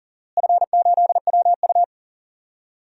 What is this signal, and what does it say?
Morse code, heard as a keyed tone, about 40 wpm, 700 Hz. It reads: F8WU
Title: DXCC-100 Call Signs - 4 Characters - Repeated 3x - 40wpm